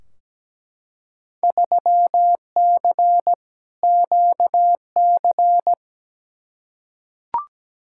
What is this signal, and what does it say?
Morse code, heard as a keyed tone, about 17 wpm, 700 Hz. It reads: 3C QC
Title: Field Day Exchanges Encore - 17wpm